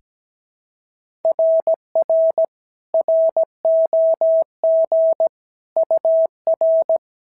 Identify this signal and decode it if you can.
Morse code, heard as a keyed tone, about 16 wpm, 650 Hz. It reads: RR ROG UR
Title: SOTA QSO Practice 17wpm